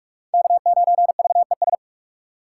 Morse code, heard as a keyed tone, about 46 wpm, 700 Hz. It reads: K9VES
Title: USA Call Signs - 5 Characters - 3x Character Spacing - 45wpm